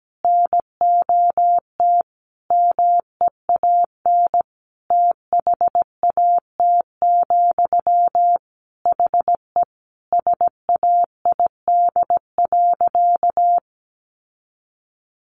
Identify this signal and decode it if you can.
Morse code, heard as a keyed tone, about 17 wpm, 700 Hz. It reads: NOT MEAN THAT, HE SAID.
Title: Sentences from Top 600 Words 17wpm